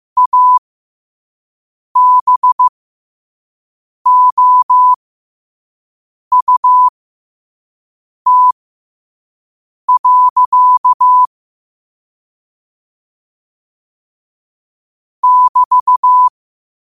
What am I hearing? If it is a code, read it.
Morse code, heard as a keyed tone, about 15 wpm, spaced as for 5 wpm, 1000 Hz. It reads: ABOUT. <BT>
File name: qotdfiles / 5wpm